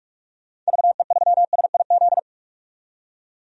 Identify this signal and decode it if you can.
Morse code, heard as a keyed tone, about 45 wpm, 700 Hz. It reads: VE3SIZ
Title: DXCC-100 Call Signs - 6 Characters - 45wpm